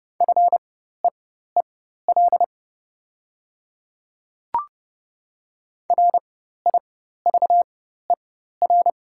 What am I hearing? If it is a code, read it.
Morse code, heard as a keyed tone, about 30 wpm, spaced as for 14 wpm, 690 Hz. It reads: FEEL RIVER
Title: 4x Character Spacing - Top 500 Words - 30wpm